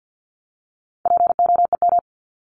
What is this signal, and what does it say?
Morse code, heard as a keyed tone, about 50 wpm, 700 Hz. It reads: POEM
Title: Top 1751-2000 Words 50wpm